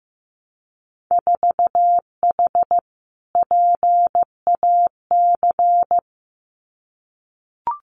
Morse code, heard as a keyed tone, about 15 wpm, 700 Hz. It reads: 4H PAC
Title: Winter Field Day Exchanges Encore 15wpm